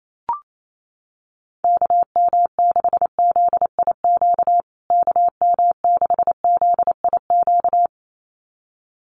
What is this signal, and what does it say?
Morse code, heard as a keyed tone, about 28 wpm, 700 Hz. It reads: KM6ZIQ KM6ZIQ
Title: USA Call Signs Repeated 2x 28wpm